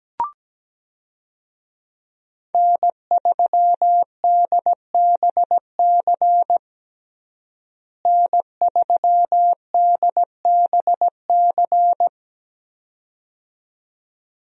Morse code, heard as a keyed tone, about 17 wpm, 700 Hz. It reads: N3DBC N3DBC
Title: USA Call Signs - 5 Characters - Repeated 2x - 17wpm